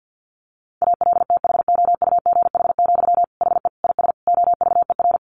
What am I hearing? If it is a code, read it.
Morse code, heard as a keyed tone, about 49 wpm, 700 Hz. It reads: ALTHOUGH, HE IS OVER
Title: Sentences from Top 600 Words 50wpm